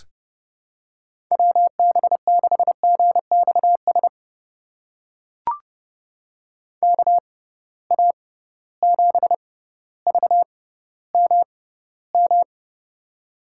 Morse code, heard as a keyed tone, about 30 wpm, spaced as for 15 wpm, 700 Hz. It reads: WB6GXS KA7VMM
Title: USA Call Signs - 6 Characters - 6x Character Spacing - 30wpm